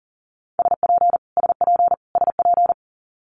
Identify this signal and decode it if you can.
Morse code, heard as a keyed tone, about 38 wpm, 700 Hz. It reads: SP SP SP
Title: Multi-letter Phonemes Repeated 3x 40wpm